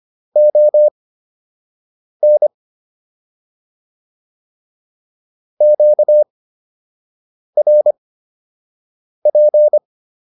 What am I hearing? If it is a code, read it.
Morse code, heard as a keyed tone, about 25 wpm, spaced as for 6 wpm, 600 Hz. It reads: ON QRP